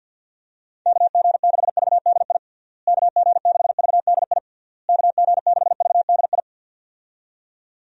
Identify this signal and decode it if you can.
Morse code, heard as a keyed tone, about 52 wpm, 700 Hz. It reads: KG6VDI KG6VDI KG6VDI
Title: USA Call Signs - 6 Characters - Repeated 3x - 50wpm